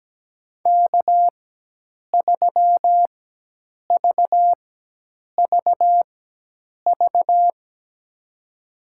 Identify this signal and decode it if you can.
Morse code, heard as a keyed tone, about 17 wpm, spaced as for 8 wpm, 700 Hz. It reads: K3VVV
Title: USA Call Signs - 5 Characters - 4x Character Spacing - 17wpm